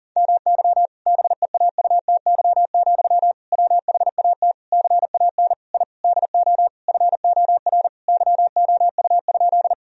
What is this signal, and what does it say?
Morse code, heard as a keyed tone, about 40 wpm, 700 Hz. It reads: MY BEAUTY, WHAT CAN I DO FOR YOU?
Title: Sentences from Top 1000 Words - Review Part 2 - 40wpm